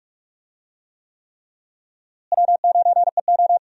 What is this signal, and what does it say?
Morse code, heard as a keyed tone, about 42 wpm, 700 Hz. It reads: W9EO
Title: USA Call Signs - 4 Characters - Repeated 3x - 45wpm